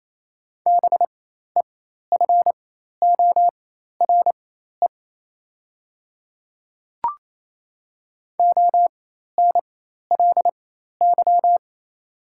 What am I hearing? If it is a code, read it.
Morse code, heard as a keyed tone, about 28 wpm, spaced as for 13 wpm, 690 Hz. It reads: BEFORE ONLY
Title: 4x Character Spacing - Top 300 Words - 28wpm